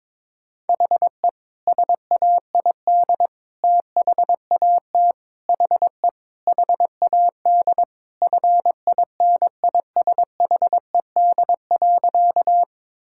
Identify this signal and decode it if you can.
Morse code, heard as a keyed tone, about 22 wpm, 700 Hz. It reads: HE SAID THAT HE HAD FINISHED.
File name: practice / 22wpm / Sentences with 3-Word Phrases - 22wpm